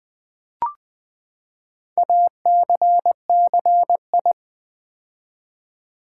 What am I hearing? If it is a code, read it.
Morse code, heard as a keyed tone, about 20 wpm, 700 Hz. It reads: ACCI
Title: Build A Word - 4 Letters At A Time - 12-Letter Words - 20wpm